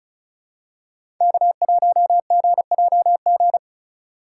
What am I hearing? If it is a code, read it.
Morse code, heard as a keyed tone, about 36 wpm, 700 Hz. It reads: K1GJG
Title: USA Call Signs - 5 Characters - 2x Character Spacing - 35wpm